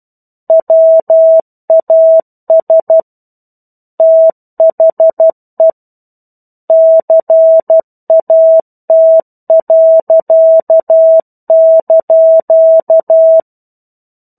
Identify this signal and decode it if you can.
Morse code, heard as a keyed tone, about 12 wpm, 650 Hz. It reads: WAS THE CAT.)
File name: 01-Chapter-01 12 WPM